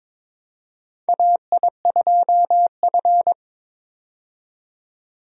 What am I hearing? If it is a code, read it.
Morse code, heard as a keyed tone, about 21 wpm, 700 Hz. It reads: AI2F
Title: USA Call Signs - 4 Characters - 22wpm